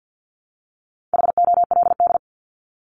Morse code, heard as a keyed tone, about 50 wpm, 700 Hz. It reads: HOLD
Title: Top 500 Words Repeated 50wpm